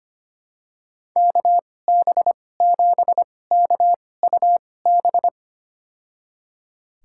Morse code, heard as a keyed tone, about 25 wpm, spaced as for 18 wpm, 700 Hz. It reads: KB7KUB
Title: USA Call Signs - 6 Characters - 2x Character Spacing - 25wpm